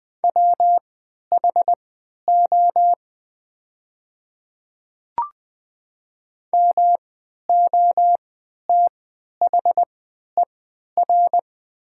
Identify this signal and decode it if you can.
Morse code, heard as a keyed tone, about 20 wpm, spaced as for 11 wpm, 700 Hz. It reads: WHO MOTHER
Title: 3x Character Spacing - Top 300 Words - 20wpm